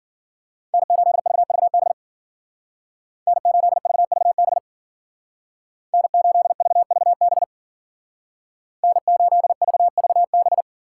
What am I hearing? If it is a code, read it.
Morse code, heard as a keyed tone, about 49 wpm, 700 Hz. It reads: N8VVB N8VVB N8VVB N8VVB
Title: Speed Racing - USA Call Signs 40wpm